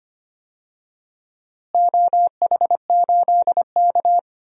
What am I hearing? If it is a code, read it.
Morse code, heard as a keyed tone, about 25 wpm, 700 Hz. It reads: OH8K
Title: DXCC-100 Call Signs - 4 Characters - Repeated 3x - 25wpm